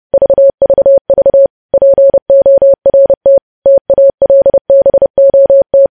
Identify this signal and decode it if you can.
Morse code, heard as a keyed tone, about 30 wpm, 550 Hz. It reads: VVV PORT TALBOT
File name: podcast / News headlines-1x-30x30